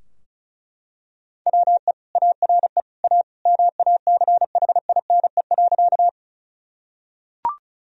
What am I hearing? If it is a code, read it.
Morse code, heard as a keyed tone, about 36 wpm, 700 Hz. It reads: WE ARE A MACHINE.
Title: Sentences from Top 1000 Words - Review Part 2 - 35wpm